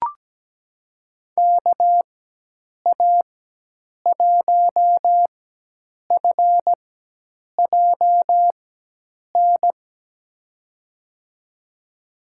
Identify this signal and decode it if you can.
Morse code, heard as a keyed tone, about 17 wpm, spaced as for 8 wpm, 700 Hz. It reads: KA1FJN
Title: USA Call Signs - 6 Characters - 4x Character Spacing - 17wpm